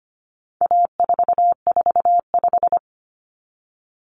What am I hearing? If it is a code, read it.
Morse code, heard as a keyed tone, about 25 wpm, 700 Hz. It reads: A445
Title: Common Cut Numbers - 4-Digit 25wpm